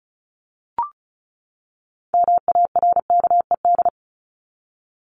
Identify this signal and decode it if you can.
Morse code, heard as a keyed tone, about 34 wpm, 700 Hz. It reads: MARKED
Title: ED Suffix - Top 100 Words with Base Words 35wpm